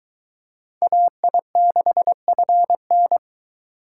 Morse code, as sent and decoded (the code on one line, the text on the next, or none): .- .. -.... ..-. -.
AI6FN